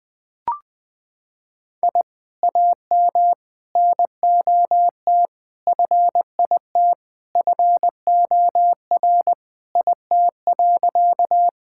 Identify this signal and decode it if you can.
Morse code, heard as a keyed tone, about 20 wpm, 700 Hz. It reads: I AM NOT FIT FOR IT.
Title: Sentences from Top 1000 Words - Review Part 5 - 20wpm